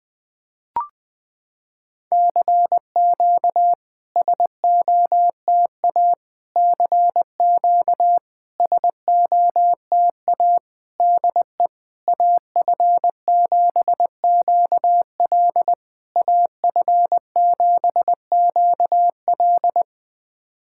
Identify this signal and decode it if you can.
Morse code, heard as a keyed tone, about 20 wpm, 700 Hz. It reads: CQ SOTA CQ SOTA DE AF7QL AF7QL
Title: SOTA QSO Practice 20wpm